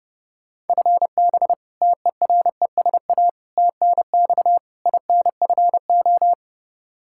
Morse code, heard as a keyed tone, about 32 wpm, 700 Hz. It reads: FB TERESA TNX INFO
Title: Practice QSO - Exchange 3 - More Info - 30wpm